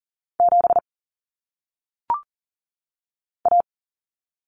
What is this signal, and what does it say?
Morse code, heard as a keyed tone, about 40 wpm, 700 Hz. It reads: 7 A
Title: Common Cut Numbers - 1-Digit 40wpm